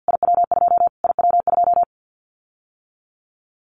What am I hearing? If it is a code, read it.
Morse code, heard as a keyed tone, about 49 wpm, 700 Hz. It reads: IW2 IW2
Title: Callsign Prefixes for Top 25 DXCC - Repeated 3x - 50wpm